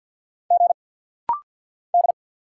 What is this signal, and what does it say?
Morse code, heard as a keyed tone, about 48 wpm, 700 Hz. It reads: G D
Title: Single Letter-Number - Rapid-Fire 50wpm